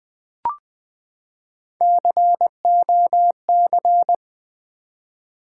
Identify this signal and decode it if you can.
Morse code, heard as a keyed tone, about 20 wpm, 700 Hz. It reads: COC
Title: Build A Word - 3 Letters At A Time - 8-Letter Words - 20wpm